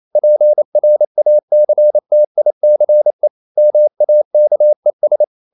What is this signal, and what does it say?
Morse code, heard as a keyed tone, about 28 wpm, 600 Hz. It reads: PRACTICE MAKES
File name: CW / Practice / EnglishSayings28